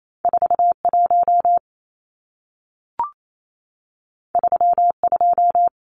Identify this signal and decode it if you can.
Morse code, heard as a keyed tone, about 28 wpm, 700 Hz. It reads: 41 32
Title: Numbers - 2-Digit 28wpm